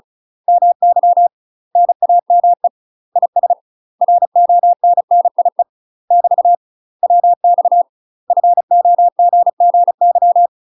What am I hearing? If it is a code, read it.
Morse code, heard as a keyed tone, about 36 wpm, 700 Hz. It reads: MY NAME IS RONNIE <BT> WX FOGGY
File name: UK MASTER QSOs TRAINING / 35WPM / UK MASTER QSOs 0009